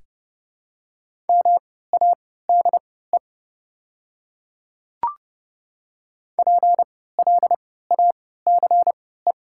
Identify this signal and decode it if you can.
Morse code, heard as a keyed tone, about 30 wpm, spaced as for 17 wpm, 700 Hz. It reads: MADE PLACE
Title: 3x Character Spacing - Top 200 Words - 30wpm